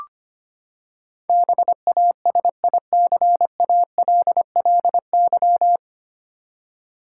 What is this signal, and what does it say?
Morse code, heard as a keyed tone, about 25 wpm, 700 Hz. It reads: BASICALLY
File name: Nine Letter Words - 25wpm